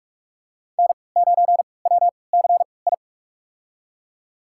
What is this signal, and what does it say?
Morse code, heard as a keyed tone, about 46 wpm, spaced as for 26 wpm, 700 Hz. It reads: N9WCI
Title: USA Call Signs - 5 Characters - 3x Character Spacing - 45wpm